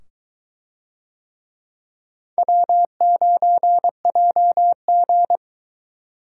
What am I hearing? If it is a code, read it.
Morse code, heard as a keyed tone, about 22 wpm, 700 Hz. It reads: W9JG